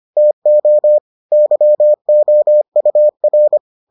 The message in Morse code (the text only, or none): TO YOUR